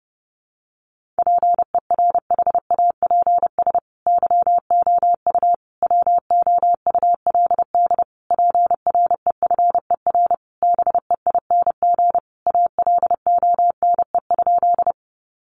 Morse code, heard as a keyed tone, about 30 wpm, 700 Hz. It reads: PERHAPS YOU WOULD PREFER BEING ALONE?
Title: Sentences from Top 1100 Words 30wpm